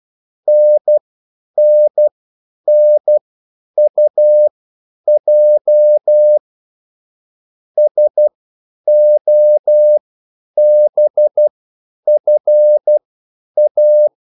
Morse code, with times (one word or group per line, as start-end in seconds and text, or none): 0.5-6.4 NNNUJ
7.8-14.1 SOBFA